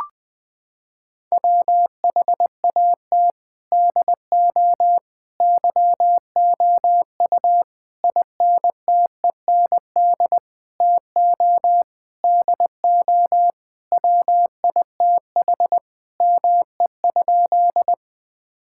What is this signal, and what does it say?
Morse code, heard as a keyed tone, about 20 wpm, 700 Hz. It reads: WHAT DO YOU INTEND TO DO WITH ME?
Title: Sentences from Top 1600 Words 20wpm